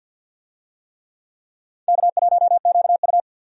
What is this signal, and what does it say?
Morse code, heard as a keyed tone, about 52 wpm, 700 Hz. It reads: K1QU
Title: USA Call Signs - 4 Characters - Repeated 3x - 50wpm